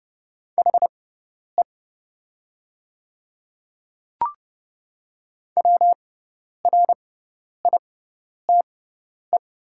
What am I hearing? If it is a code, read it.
Morse code, heard as a keyed tone, about 30 wpm, spaced as for 10 wpm, 690 Hz. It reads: HE WRITE
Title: 6x Character Spacing - Top 100 Words - 30wpm